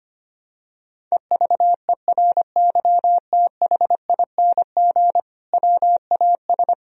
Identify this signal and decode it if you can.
Morse code, heard as a keyed tone, about 25 wpm, 700 Hz. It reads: EVERYTHING WAS
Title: Sentences from Top 1000 Words - Review Part 2 - 25wpm